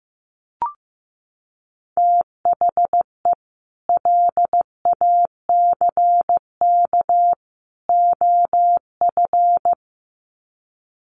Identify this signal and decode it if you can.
Morse code, heard as a keyed tone, about 15 wpm, 700 Hz. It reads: THE LACK OF
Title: Sets of 3 Words 15wpm